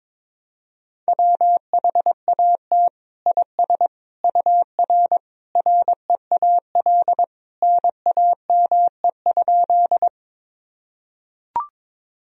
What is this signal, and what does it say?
Morse code, heard as a keyed tone, about 22 wpm, 700 Hz. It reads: WHAT IS UR REAL NAME?
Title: Sentences from Top 300 Words and Abbreviations 22wpm